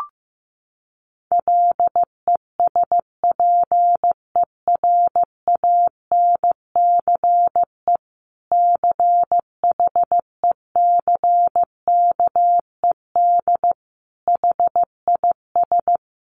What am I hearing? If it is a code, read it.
Morse code, heard as a keyed tone, about 15 wpm, 700 Hz. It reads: LESPERANCE CHECKED HIS
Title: A Sound Of Thunder 15wpm